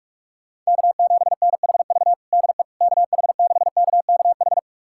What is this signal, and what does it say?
Morse code, heard as a keyed tone, about 47 wpm, 700 Hz. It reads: K7NHV DE KH6KKH
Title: USA Callsign DE Callsign 45wpm